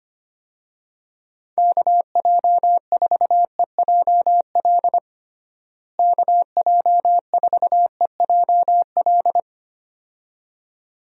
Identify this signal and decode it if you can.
Morse code, heard as a keyed tone, about 25 wpm, 700 Hz. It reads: KJ4EJL KJ4EJL
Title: USA Call Signs - 6 Characters - Repeated 2x - 25wpm